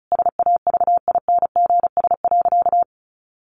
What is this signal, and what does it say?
Morse code, heard as a keyed tone, about 34 wpm, 700 Hz. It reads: SAVINGS.
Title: Sentences from Top 1400 Words 35wpm